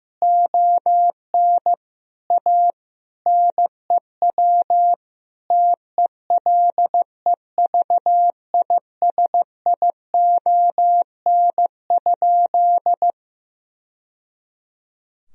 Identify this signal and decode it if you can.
Morse code, heard as a keyed tone, about 15 wpm, 700 Hz. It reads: ON A NEW TELEVISION?
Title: Sentences from Top 1000 Words - Review Part 4 - 15wpm